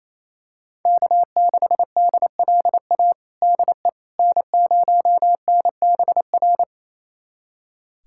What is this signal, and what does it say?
Morse code, heard as a keyed tone, about 27 wpm, 700 Hz. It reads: K6DLA DE N0NBR
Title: USA Callsign DE Callsign - 5 Characters - 28wpm